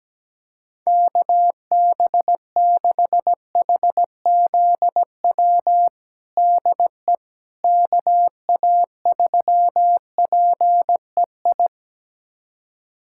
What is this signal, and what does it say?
Morse code, heard as a keyed tone, about 17 wpm, 700 Hz. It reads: KB6HZW DE KA3PEI
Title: USA Callsign DE Callsign - 6 Characters - 17wpm